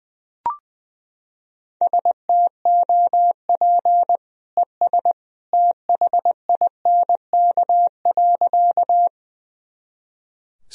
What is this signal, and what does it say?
Morse code, heard as a keyed tone, about 21 wpm, 700 Hz. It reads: STOP ES THINK.
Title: Sentences from Top 300 Words and Abbreviations 20wpm